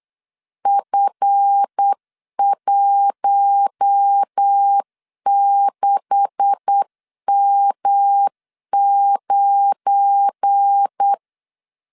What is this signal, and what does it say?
Morse code, heard as a keyed tone, about 8 wpm, 800 Hz. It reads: F16M9
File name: LEZIONI / 9 286 8wpm 3425